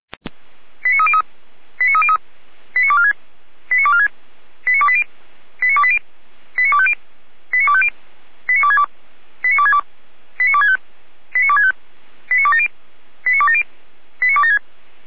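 Five-tone selective call (ZVEI), heard as quick runs of five tones, tones of about 70 ms. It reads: 89303 89303 89246 89246 89280 89280 89250 89250 89262 89262 89265 89265 89270 89270 89276
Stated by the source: ZVEI five-tone selective call received on an SDR.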